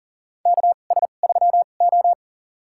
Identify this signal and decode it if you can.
Morse code, heard as a keyed tone, about 40 wpm, spaced as for 29 wpm, 690 Hz. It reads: KS3O
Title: USA Call Signs - 2x Character Spacing - 40wpm